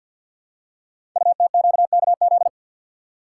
Morse code, heard as a keyed tone, about 50 wpm, 700 Hz. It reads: UTQKZ
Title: Code Groups - 5 Letters - 50wpm